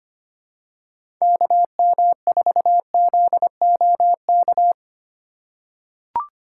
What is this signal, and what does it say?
Morse code, heard as a keyed tone, about 25 wpm, 700 Hz. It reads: KM4ZOK
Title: USA Call Signs - 6 Characters - 2x Character Spacing - 25wpm